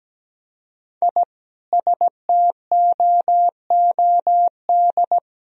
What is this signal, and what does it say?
Morse code, heard as a keyed tone, about 17 wpm, 700 Hz. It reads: I STOOD
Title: Sentences from Top 1000 Words - Review Part 2 - 17wpm